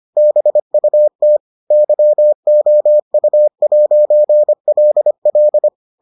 Morse code, heard as a keyed tone, about 25 wpm, 600 Hz. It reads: BUT YOU'LL